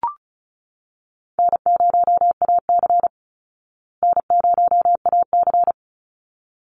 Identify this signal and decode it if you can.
Morse code, heard as a keyed tone, about 34 wpm, 700 Hz. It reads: N0AC N0AC
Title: Call Signs Repeated 3x - 35wpm